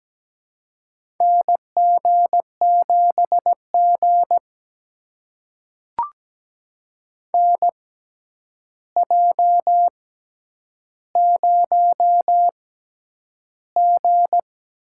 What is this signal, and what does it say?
Morse code, heard as a keyed tone, about 17 wpm, spaced as for 9 wpm, 700 Hz. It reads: NG7G NJ0G
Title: USA Call Signs - 4 Characters - 6x Character Spacing - 17wpm